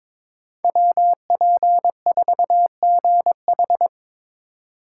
Morse code, heard as a keyed tone, about 22 wpm, 700 Hz. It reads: WP4GH
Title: USA Call Signs - 2x Character Spacing - 22wpm